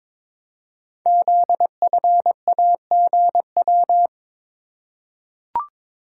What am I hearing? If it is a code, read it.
Morse code, heard as a keyed tone, about 22 wpm, 690 Hz. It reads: ZFAGW
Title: Code Groups - 5 Letters - 22wpm